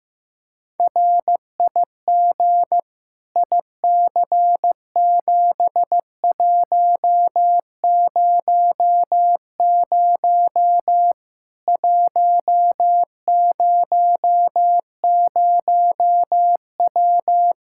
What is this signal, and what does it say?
Morse code, heard as a keyed tone, about 15 wpm, 700 Hz. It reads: RIG IC7100 100W
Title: Practice QSO - Exchange 2 - Info - 15wpm